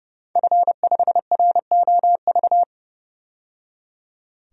Morse code, heard as a keyed tone, about 31 wpm, 700 Hz. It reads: F5ROV
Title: DXCC-100 Call Signs - 5 Characters - 30wpm